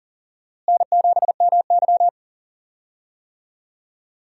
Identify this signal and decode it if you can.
Morse code, heard as a keyed tone, about 38 wpm, 700 Hz. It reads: N7MY
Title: USA Call Signs - 4 Characters - Repeated 2x - 40wpm